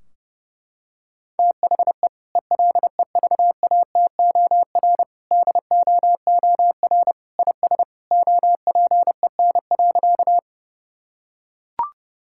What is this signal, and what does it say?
Morse code, heard as a keyed tone, about 30 wpm, 700 Hz. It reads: THE ELEVATOR DOOR IS OPEN.
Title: Sentences from Top 1600 Words 30wpm